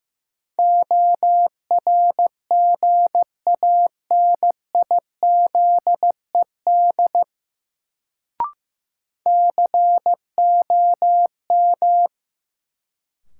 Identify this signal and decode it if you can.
Morse code, heard as a keyed tone, about 15 wpm, 700 Hz. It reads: ORGANIZED COM
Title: Sentences from Top 1300 Words - New Word Drill Alt - 15wpm